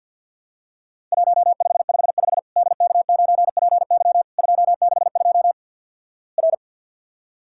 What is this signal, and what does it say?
Morse code, heard as a keyed tone, about 53 wpm, 700 Hz. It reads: 1555 DK9PY 262 R
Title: Worked All Europe DX Contest - QTC Exchange with Repeats - 50wpm